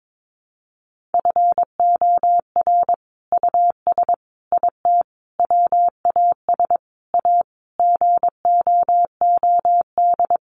Morse code, heard as a keyed tone, about 22 wpm, 700 Hz. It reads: FOR US IT WAS A GOOD
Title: Sentences from Top 1200 Words 22wpm